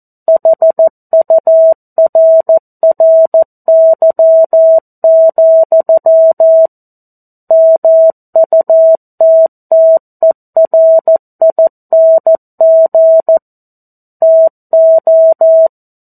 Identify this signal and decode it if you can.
Morse code, heard as a keyed tone, about 14 wpm, 650 Hz. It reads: HURRY, MUTTERING TO
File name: 02-Chapter-02 14 WPM